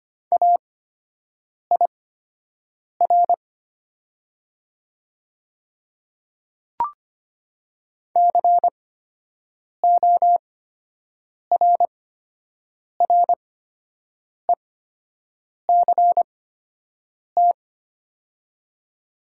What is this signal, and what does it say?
Morse code, heard as a keyed tone, about 25 wpm, spaced as for 7 wpm, 700 Hz. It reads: AIR CORRECT
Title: 8x Character Spacing - Top 500 Words - 25wpm